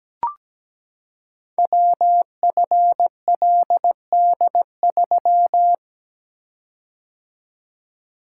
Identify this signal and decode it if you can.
Morse code, heard as a keyed tone, about 17 wpm, 700 Hz. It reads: WFLD3
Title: Code Groups - 5 Letter:Numbers - 17wpm